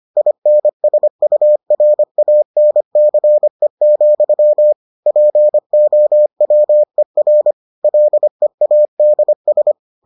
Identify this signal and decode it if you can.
Morse code, heard as a keyed tone, about 25 wpm, 600 Hz. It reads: INSURANCE, POWER LEADS